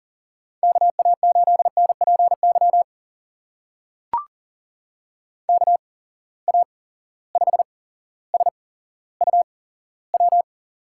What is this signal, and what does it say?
Morse code, heard as a keyed tone, about 40 wpm, spaced as for 17 wpm, 700 Hz. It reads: KA8NPY KA5SUW
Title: USA Call Signs - 6 Characters - 8x Character Spacing - 40wpm